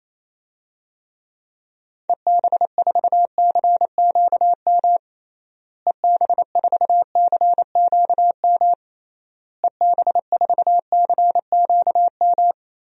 Morse code, heard as a keyed tone, about 28 wpm, 700 Hz. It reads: EB4CQM EB4CQM EB4CQM
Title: DXCC-100 Call Signs - 6 Characters - Repeated 3x - 28wpm